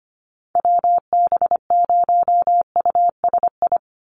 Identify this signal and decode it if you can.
Morse code, heard as a keyed tone, about 25 wpm, 700 Hz. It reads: WB0USI
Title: NA SOTA Chaser Callsigns 25wpm